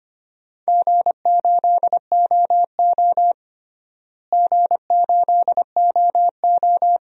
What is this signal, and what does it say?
Morse code, heard as a keyed tone, about 25 wpm, 700 Hz. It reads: G8OO G8OO
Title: DXCC-100 Call Signs - 4 Characters - Repeated 2x - 25wpm